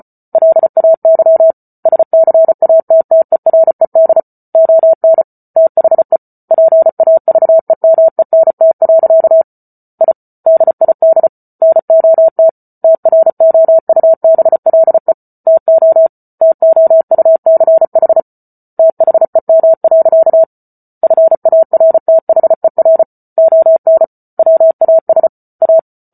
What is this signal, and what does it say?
Morse code, heard as a keyed tone, about 34 wpm, 650 Hz. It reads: LAY SCATTERED ON THE PAVEMENT. I DID NOT TROUBLE TO TOUCH THEM. FARTHER ON WAS A